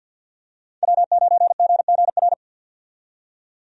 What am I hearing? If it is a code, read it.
Morse code, heard as a keyed tone, about 46 wpm, 700 Hz. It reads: W9GGR
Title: USA Call Signs - 5 Characters - 50wpm